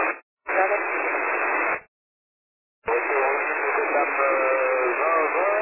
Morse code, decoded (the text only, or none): none